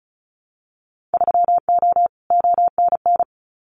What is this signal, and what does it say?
Morse code, heard as a keyed tone, about 36 wpm, 700 Hz. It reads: 3O ONN
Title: Winter Field Day Exchanges 35wpm